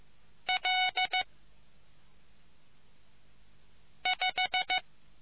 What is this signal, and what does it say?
Morse code, heard as a keyed tone, about 15 wpm, 740 Hz. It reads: L 5